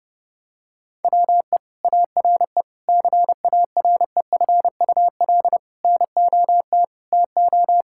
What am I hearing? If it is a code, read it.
Morse code, heard as a keyed tone, about 32 wpm, 700 Hz. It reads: WE ARE CAREFUL NOT TO
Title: Sentences from Top 1600 Words 30wpm